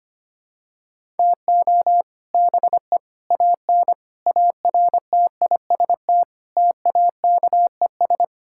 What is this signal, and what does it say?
Morse code, heard as a keyed tone, about 25 wpm, 700 Hz. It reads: TO BE AN ARTIST TAKES
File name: practice / 25wpm / Sentences with 3-Word Phrases - 25wpm